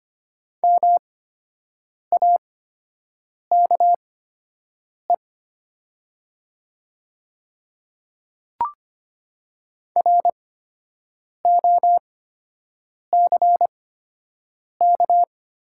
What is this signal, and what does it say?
Morse code, heard as a keyed tone, about 25 wpm, spaced as for 7 wpm, 700 Hz. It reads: MAKE ROCK
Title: 8x Character Spacing - Top 500 Words - 25wpm